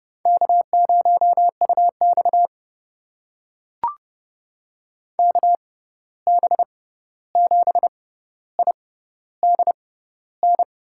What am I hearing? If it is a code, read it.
Morse code, heard as a keyed tone, about 30 wpm, spaced as for 14 wpm, 700 Hz. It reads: K0UX KB7IDN
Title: USA Call Signs - 6x Character Spacing - 30wpm